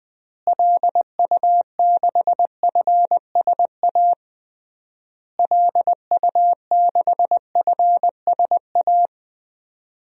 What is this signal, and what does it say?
Morse code, heard as a keyed tone, about 20 wpm, 700 Hz. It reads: LU6FSA LU6FSA
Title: DXCC-100 Call Signs - 6 Characters - Repeated 3x - 20wpm